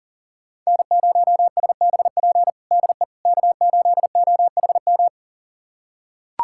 Morse code, heard as a keyed tone, about 37 wpm, 700 Hz. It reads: N0SBP DE K8OHM
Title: USA Callsign DE Callsign - 5 Characters - 40wpm